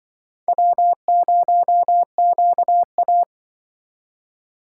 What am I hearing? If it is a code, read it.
Morse code, heard as a keyed tone, about 25 wpm, 700 Hz. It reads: W0QA